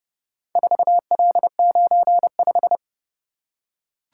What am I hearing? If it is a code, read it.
Morse code, heard as a keyed tone, about 30 wpm, 700 Hz. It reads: 4L95